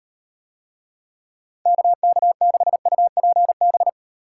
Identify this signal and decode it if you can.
Morse code, heard as a keyed tone, about 35 wpm, 700 Hz. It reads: KK6UPB